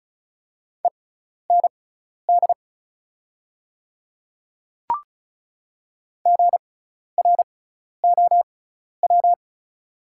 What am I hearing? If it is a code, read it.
Morse code, heard as a keyed tone, about 36 wpm, spaced as for 12 wpm, 690 Hz. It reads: END GROW
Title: 6x Character Spacing - Top 300 Words - 35wpm